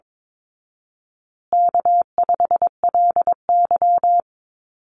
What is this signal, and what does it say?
Morse code, heard as a keyed tone, about 22 wpm, 700 Hz. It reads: K5LY
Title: Call Signs Repeated 2x - 22wpm